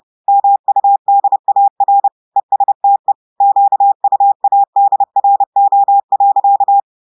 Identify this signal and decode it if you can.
Morse code, heard as a keyed tone, about 30 wpm, 800 Hz. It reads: MUDAR ESTE QUADRO.
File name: audios / portugais / portugais texte en clair 1 vitesse 30 mots minutes